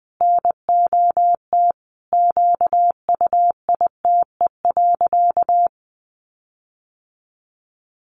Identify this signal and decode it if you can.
Morse code, heard as a keyed tone, about 20 wpm, 700 Hz. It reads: NOT QUITE.
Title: Sentences from Top 1300 Words 20wpm